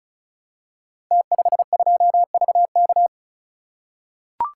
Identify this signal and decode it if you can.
Morse code, heard as a keyed tone, about 36 wpm, 700 Hz. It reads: T52VK
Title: Code Groups - 5 Letter:Numbers - 35wpm